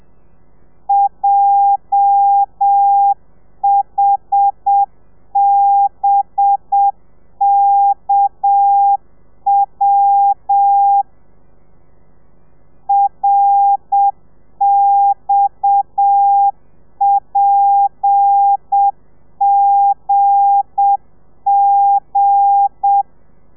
Morse code, heard as a keyed tone, about 7 wpm, 800 Hz. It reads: JHBKW RXPGG
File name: telegrafia / i2tfj / Lez8 30